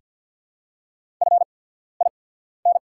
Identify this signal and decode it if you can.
Morse code, heard as a keyed tone, about 49 wpm, spaced as for 14 wpm, 700 Hz. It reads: FIN
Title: 8x Character Spacing - Top 200 Words - 50wpm